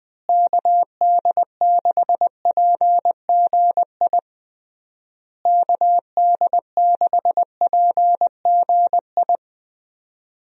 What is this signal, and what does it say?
Morse code, heard as a keyed tone, about 21 wpm, 700 Hz. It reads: KD6PGI KD6PGI
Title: USA Call Signs - 6 Characters - Repeated 3x - 20wpm